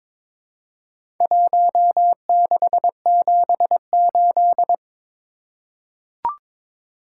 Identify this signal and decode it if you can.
Morse code, heard as a keyed tone, about 22 wpm, 700 Hz. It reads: 1678
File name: practice / 22wpm / Four Letter-Number Combinations - 22wpm